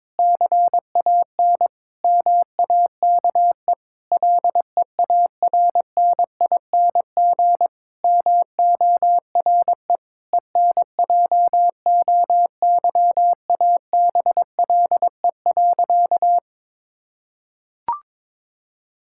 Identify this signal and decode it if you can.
Morse code, heard as a keyed tone, about 23 wpm, 700 Hz. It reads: CAN MAKE LEARNING MORE ENJOYABLE.
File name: Difficult Sentences with Hints - 22wpm